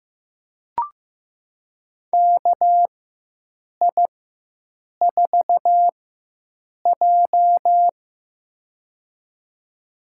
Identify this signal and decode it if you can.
Morse code, heard as a keyed tone, about 15 wpm, spaced as for 7 wpm, 700 Hz. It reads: KI4J
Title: USA Call Signs - 4 Characters - 4x Character Spacing - 15wpm